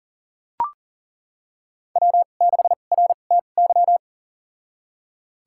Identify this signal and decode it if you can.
Morse code, heard as a keyed tone, about 40 wpm, spaced as for 29 wpm, 700 Hz. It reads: W6RTY
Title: USA Call Signs - 5 Characters - 2x Character Spacing - 40wpm